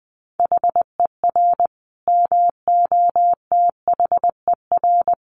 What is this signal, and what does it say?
Morse code, heard as a keyed tone, about 20 wpm, 700 Hz. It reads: HER MOTHER
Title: Sentences from Top 1300 Words 20wpm